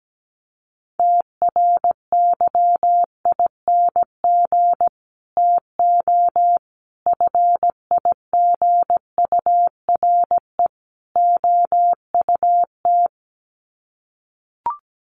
Morse code, heard as a keyed tone, about 17 wpm, 700 Hz. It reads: TRYING TO FIGURE OUT
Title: Sets of 4 Words 17wpm